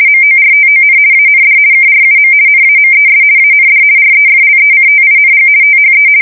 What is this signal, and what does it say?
RTTY, about 50 baud. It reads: MORTON AND KRUM NAMES AND REFLECTING THE
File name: fsk170-50B / teletype-corp